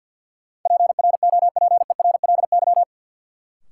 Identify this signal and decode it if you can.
Morse code, heard as a keyed tone, about 52 wpm, 700 Hz. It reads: PROPERLY
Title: Sentences from Top 1100 Words - New Word Drill - 50wpm